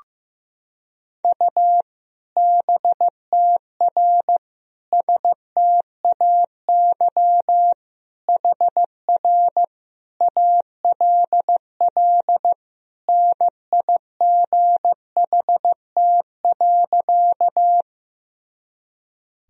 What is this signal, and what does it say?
Morse code, heard as a keyed tone, about 15 wpm, 700 Hz. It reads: U BTR STAY HR ALL NIGHT.
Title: Sentences from Top 500 Words and Abbreviations 15wpm